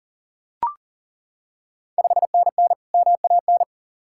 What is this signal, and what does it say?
Morse code, heard as a keyed tone, about 39 wpm, 700 Hz. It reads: 5NN MAN
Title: Ontario QSO Party Exchange 40wpm